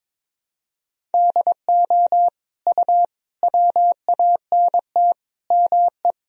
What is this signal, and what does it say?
Morse code, heard as a keyed tone, about 22 wpm, 700 Hz. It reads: DO U WANT ME
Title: Sentences from Top 200 Words and Abbreviations 22wpm